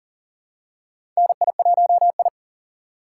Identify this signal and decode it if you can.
Morse code, heard as a keyed tone, about 40 wpm, 700 Hz. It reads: NI1I
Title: USA Call Signs - 4 Characters - 6x Character Spacing - 40wpm